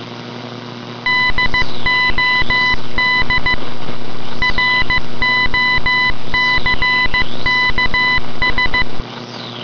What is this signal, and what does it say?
Morse code, heard as a keyed tone, about 15 wpm, 2000 Hz. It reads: DOD ROCKS
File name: ambience / radio morse